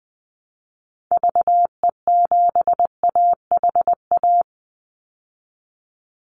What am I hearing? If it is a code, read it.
Morse code, heard as a keyed tone, about 20 wpm, 700 Hz. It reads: VE7AHA
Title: Call Signs Repeated 2x - 20wpm